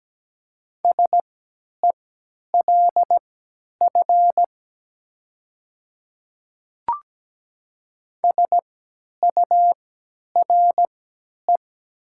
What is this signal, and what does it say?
Morse code, heard as a keyed tone, about 17 wpm, spaced as for 10 wpm, 700 Hz. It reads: SELF SURE
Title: 3x Character Spacing - Top 500 Words - 17wpm